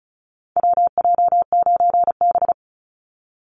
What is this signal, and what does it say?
Morse code, heard as a keyed tone, about 34 wpm, 700 Hz. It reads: WJ9B
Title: Call Signs - 35wpm